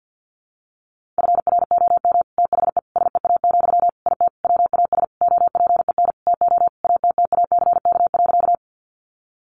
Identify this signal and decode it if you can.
Morse code, heard as a keyed tone, about 50 wpm, 700 Hz. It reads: FROM THE SEA, IT WAS OPEN TO ATTACK.
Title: Sentences from Top 1500 Words 50wpm